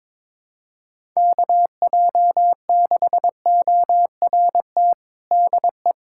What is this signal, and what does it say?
Morse code, heard as a keyed tone, about 22 wpm, 700 Hz. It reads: KJ6ORT DE
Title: USA Callsign DE Callsign - 6 Characters - 22wpm